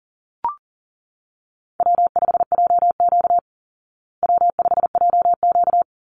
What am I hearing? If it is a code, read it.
Morse code, heard as a keyed tone, about 40 wpm, 700 Hz. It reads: W5JQ W5JQ
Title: Call Signs Repeated 3x - 40wpm